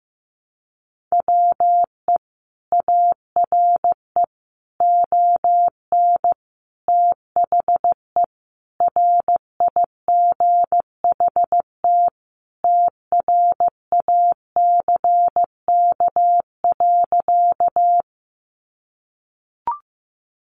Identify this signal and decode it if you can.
Morse code, heard as a keyed tone, about 15 wpm, 700 Hz. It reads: WE ARE ON THE RIGHT TRACK.
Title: Sentences from Top 1100 Words 15wpm